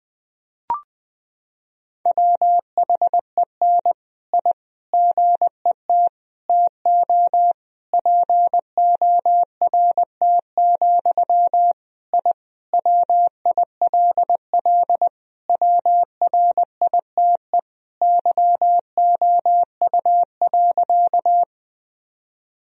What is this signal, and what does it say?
Morse code, heard as a keyed tone, about 21 wpm, 700 Hz. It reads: WHEN I GET TO PORT, I WILL WRITE YOU.
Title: Sentences from Top 1000 Words - Review Part 1 - 20wpm